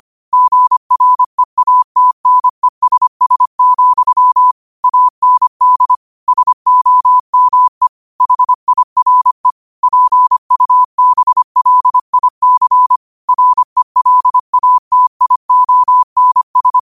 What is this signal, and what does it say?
Morse code, heard as a keyed tone, about 25 wpm, 1000 Hz. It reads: GREATNESS, AND SOME HIRE PUBLIC RELATIONS
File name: qotdfiles / 25wpm